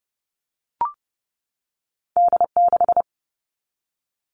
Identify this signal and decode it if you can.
Morse code, heard as a keyed tone, about 30 wpm, 700 Hz. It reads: D6
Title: All Cut Numbers - 2-Digit 30wpm